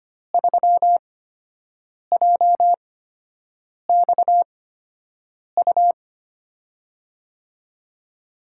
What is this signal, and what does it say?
Morse code, heard as a keyed tone, about 25 wpm, spaced as for 7 wpm, 700 Hz. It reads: 3JXU
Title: USA Call Signs - 5 Characters - 8x Character Spacing - 25wpm